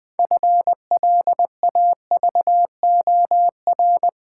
Morse code, heard as a keyed tone, about 21 wpm, 700 Hz. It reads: FLAVOR
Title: Build A Word - 3 Letters At A Time - 6-Letter Words - 20wpm